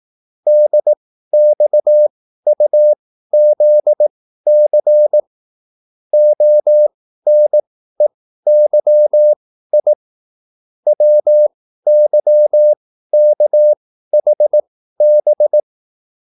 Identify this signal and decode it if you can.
Morse code, heard as a keyed tone, about 18 wpm, spaced as for 13 wpm, 600 Hz. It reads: DXUZC ONEYI WYKHB
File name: QSL / CW lesson